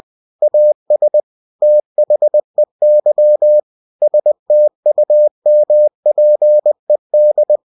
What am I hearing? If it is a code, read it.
Morse code, heard as a keyed tone, about 20 wpm, 600 Hz. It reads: AS THEY STUMPED
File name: cw-books / winnie-the-pooh / winnie-20wpm-page-0034